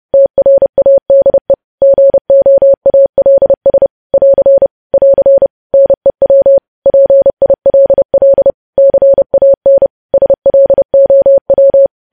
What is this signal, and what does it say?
Morse code, heard as a keyed tone, about 30 wpm, 550 Hz. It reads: TRADE GOALS <AR> <AR> NEW PILL CAN SLOW
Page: news headlines in morse code at 30 wpm